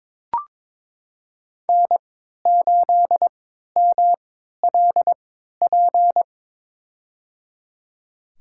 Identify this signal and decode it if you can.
Morse code, heard as a keyed tone, about 22 wpm, spaced as for 13 wpm, 700 Hz. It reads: N8MLP
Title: USA Call Signs - 3x Character Spacing - 22wpm